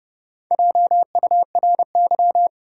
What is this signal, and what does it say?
Morse code, heard as a keyed tone, about 30 wpm, 700 Hz. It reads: JURY
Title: Four Letter Words - 30wpm